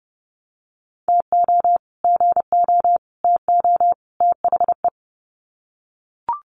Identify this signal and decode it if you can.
Morse code, heard as a keyed tone, about 30 wpm, 700 Hz. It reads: TO GO TO THE
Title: Sets of 4 Words 30wpm